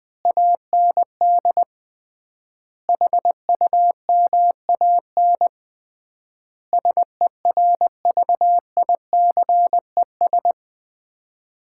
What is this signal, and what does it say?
Morse code, heard as a keyed tone, about 20 wpm, 700 Hz. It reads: AND HUMAN SERVICES
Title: 3x Word-Spacing - Sets of 5 Words - 20wpm